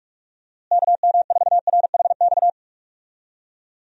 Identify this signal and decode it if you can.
Morse code, heard as a keyed tone, about 45 wpm, 700 Hz. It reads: KM4RHX
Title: USA Call Signs - 6 Characters - Repeated 3x - 45wpm